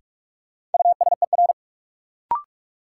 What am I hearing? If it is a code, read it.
Morse code, heard as a keyed tone, about 46 wpm, 700 Hz. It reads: USER
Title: Sentences from Top 1000 Words - Review Part 5 - New Word Drill - 45wpm